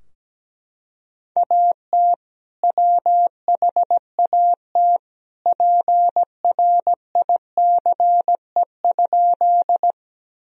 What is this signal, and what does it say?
Morse code, heard as a keyed tone, about 17 wpm, 700 Hz. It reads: AT WHAT PRICE?
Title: Sentences from Top 1000 Words - Review Part 4 - 17wpm